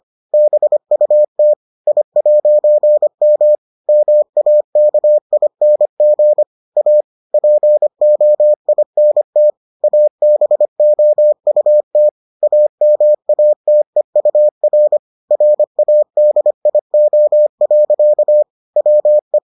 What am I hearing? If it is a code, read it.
Morse code, heard as a keyed tone, about 25 wpm, 600 Hz. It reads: BUT I'M MAKING A POINT ABOUT AMATEUR RADIO. WE